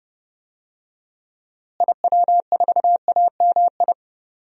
Morse code, heard as a keyed tone, about 31 wpm, 700 Hz. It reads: IW4AMI
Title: DXCC-100 Call Signs - 6 Characters - Repeated 3x - 30wpm